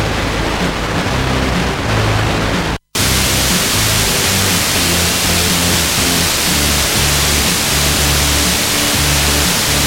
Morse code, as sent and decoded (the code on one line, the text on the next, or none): none